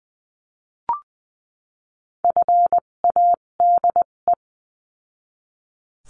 Morse code, heard as a keyed tone, about 20 wpm, spaced as for 17 wpm, 700 Hz. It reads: FADE
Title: 257 Lesson 42b - Words 14-20wpm